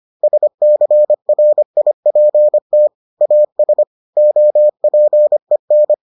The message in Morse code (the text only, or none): SCRIPT AS OPEN